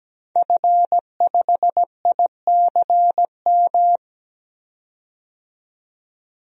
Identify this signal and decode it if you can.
Morse code, heard as a keyed tone, about 17 wpm, 700 Hz. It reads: F5ICM
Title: DXCC-100 Call Signs - 5 Characters - 17wpm